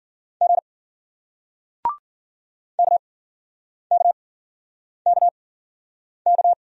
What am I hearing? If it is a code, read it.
Morse code, heard as a keyed tone, about 50 wpm, spaced as for 9 wpm, 700 Hz. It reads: G KKKK